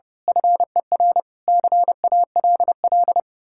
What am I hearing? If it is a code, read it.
Morse code, heard as a keyed tone, about 28 wpm, 700 Hz. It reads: FER CALL
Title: Speed-Racing Common QSO Phrases Encore - 30wpm